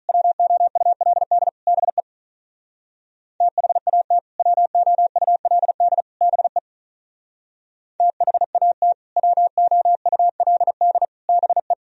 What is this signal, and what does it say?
Morse code, heard as a keyed tone, about 46 wpm, 690 Hz. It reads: WOULD BE THAT WOULD BE THAT WOULD BE